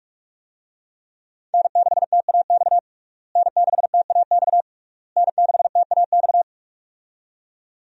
Morse code, heard as a keyed tone, about 46 wpm, 700 Hz. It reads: N6TAX N6TAX N6TAX
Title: USA Call Signs - 5 Characters - Repeated 3x - 45wpm